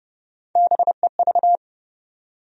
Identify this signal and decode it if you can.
Morse code, heard as a keyed tone, about 30 wpm, 700 Hz. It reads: BEV
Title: USA Common Names - Max 3 Letters - 30wpm